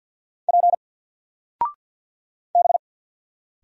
Morse code, heard as a keyed tone, about 50 wpm, 700 Hz. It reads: P B
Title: Single Letter-Number 50wpm